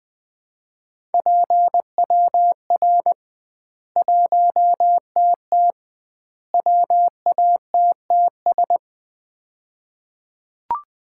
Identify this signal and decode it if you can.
Morse code, heard as a keyed tone, about 20 wpm, 700 Hz. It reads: PWR 1TT WATTS
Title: Common QSO Phrases - 2x Word-Spacing - 20wpm